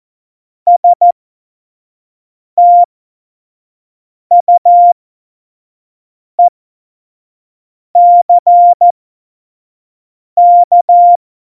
Morse code, heard as a keyed tone, about 14 wpm, spaced as for 5 wpm, 700 Hz. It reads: STUECK
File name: Morsekurs / Witze / Witz 05. Januar 2021, 25 F70